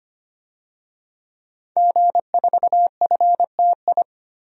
Morse code, heard as a keyed tone, about 25 wpm, 700 Hz. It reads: G4FTI
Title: DXCC-100 Call Signs - 5 Characters - Repeated 3x - 25wpm